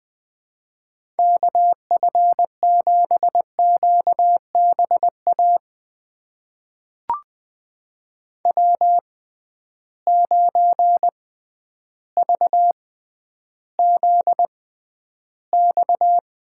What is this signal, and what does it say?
Morse code, heard as a keyed tone, about 20 wpm, spaced as for 11 wpm, 700 Hz. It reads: KF7QBA W9VZX
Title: USA Call Signs - 6x Character Spacing - 20wpm